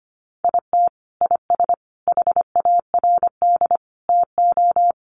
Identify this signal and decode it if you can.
Morse code, heard as a keyed tone, about 25 wpm, 700 Hz. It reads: IT IS HARD TO
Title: Sets of 5 Words Encore2 25wpm